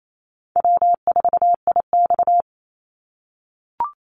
Morse code, heard as a keyed tone, about 28 wpm, 700 Hz. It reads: W4IX
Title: Easy Call Signs - 28wpm